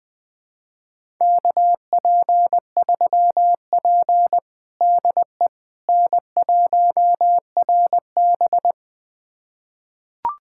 Morse code, heard as a keyed tone, about 21 wpm, 700 Hz. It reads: KP3P DE N1RB
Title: USA Callsign DE Callsign - 4 Characters - 20wpm